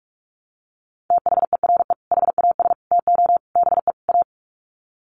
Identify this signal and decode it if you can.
Morse code, heard as a keyed tone, about 44 wpm, 700 Hz. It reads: THERE HAS TO BE A
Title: Sets of 5 Words Encore 45wpm